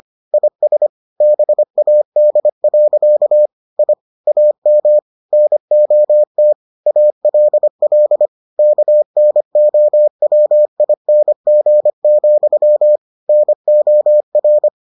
Morse code, heard as a keyed tone, about 25 wpm, 600 Hz. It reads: IS BAD. I AM NOT ALL KNOWING, NOR